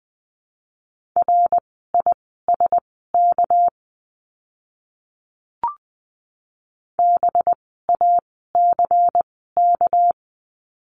Morse code, heard as a keyed tone, about 20 wpm, spaced as for 14 wpm, 700 Hz. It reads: RISK BACK
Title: 199 Lesson 32f - Letter k - Words 20wpm